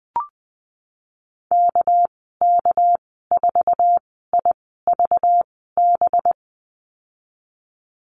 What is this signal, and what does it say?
Morse code, heard as a keyed tone, about 19 wpm, spaced as for 14 wpm, 700 Hz. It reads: KK4IVB
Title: 213 Lesson 34g - Number 8 - Call Signs 20wpm